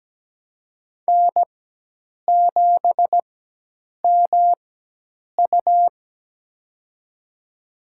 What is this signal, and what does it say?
Morse code, heard as a keyed tone, about 17 wpm, spaced as for 8 wpm, 700 Hz. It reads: N7MU
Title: USA Call Signs - 4x Character Spacing - 17wpm